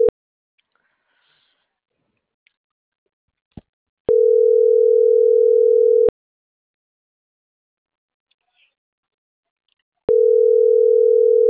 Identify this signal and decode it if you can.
Ringback tone (North America), 440 + 480 Hz.